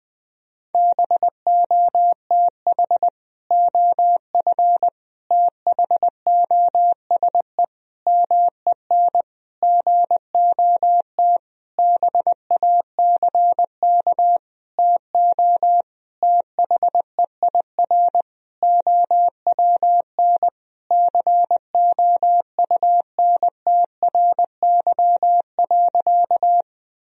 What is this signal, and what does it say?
Morse code, heard as a keyed tone, about 21 wpm, 700 Hz. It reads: BOTH OF THOSE MEN GOT BACK TO THEIR OWN COUNTRY.
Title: Sentences from Top 1000 Words - Review Part 1 - 20wpm